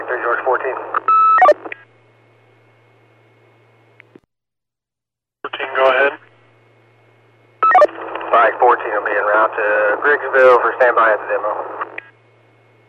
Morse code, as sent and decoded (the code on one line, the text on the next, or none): none